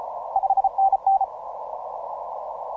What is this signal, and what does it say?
Morse code, heard as a keyed tone, about 33 wpm, 780 Hz, with radio noise. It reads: 5NN